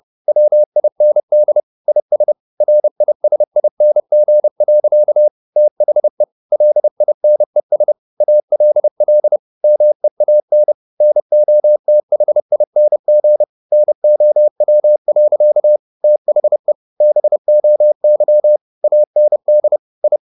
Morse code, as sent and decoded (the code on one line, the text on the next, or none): .-- .. -. -.. / .. ... / .-. .. ... .. -. --. .-.-.- / - .... / .-.. .. -. ... / .- .-.. .-.. / -- . .- -. / -. --- - .... .. -. --. / -. --- .-- .-.-.- / - .... / -... --- -.-- / .- -. -.. / ..
WIND IS RISING. THE LINES ALL MEAN NOTHING NOW. THE BOY AND I